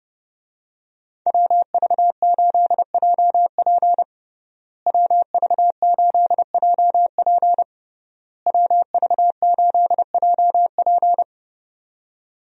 Morse code, heard as a keyed tone, about 30 wpm, 700 Hz. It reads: WV8JP WV8JP WV8JP
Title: USA Call Signs - 5 Characters - Repeated 3x - 30wpm